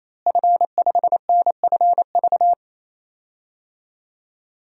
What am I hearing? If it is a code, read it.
Morse code, heard as a keyed tone, about 27 wpm, 700 Hz. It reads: F5NFV
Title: DXCC-100 Call Signs - 5 Characters - Repeated 3x - 28wpm